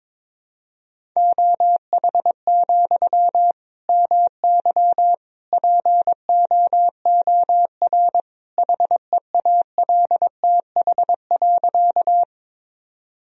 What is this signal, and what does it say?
Morse code, heard as a keyed tone, about 21 wpm, 700 Hz. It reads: OH, MY POOR HEALTH.
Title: Sentences from Top 1000 Words - Review Part 3 - 22wpm